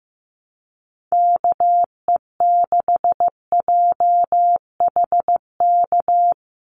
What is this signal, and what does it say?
Morse code, heard as a keyed tone, about 15 wpm, 700 Hz. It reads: KE6JHK
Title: USA Call Signs 15wpm